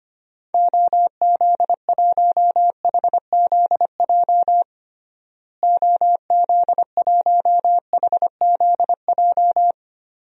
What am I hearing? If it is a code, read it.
Morse code, heard as a keyed tone, about 25 wpm, 700 Hz. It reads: OZ1HZJ OZ1HZJ
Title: DXCC-100 Call Signs - 6 Characters - Repeated 3x - 25wpm